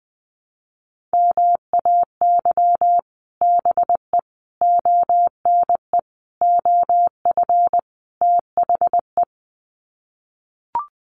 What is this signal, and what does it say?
Morse code, heard as a keyed tone, about 20 wpm, 700 Hz. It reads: MAY BE ONE OF THE
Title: Sets of 5 Words Encore2 20wpm